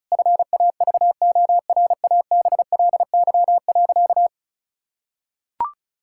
Morse code, heard as a keyed tone, about 36 wpm, 700 Hz. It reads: FAVORABLY.
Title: Pangram Challenge v2 - 35wpm